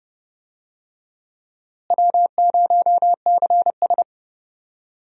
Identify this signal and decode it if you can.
Morse code, heard as a keyed tone, about 30 wpm, 700 Hz. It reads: W0CS
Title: USA Call Signs - 4 Characters - Repeated 2x - 30wpm